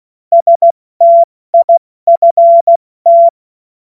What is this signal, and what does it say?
Morse code, heard as a keyed tone, about 16 wpm, 680 Hz. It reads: STIFT
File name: Morsekurs / Witze / Witz 20. Januar 2026 70 F80